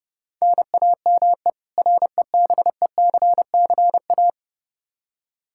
Morse code, heard as a keyed tone, about 30 wpm, 700 Hz. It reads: NAME REBECCA
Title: K1USN SST QSO Practice 30wpm